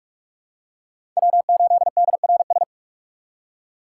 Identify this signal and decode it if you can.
Morse code, heard as a keyed tone, about 48 wpm, 700 Hz. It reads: W8DRS
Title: USA Call Signs - 4x Character Spacing - 45wpm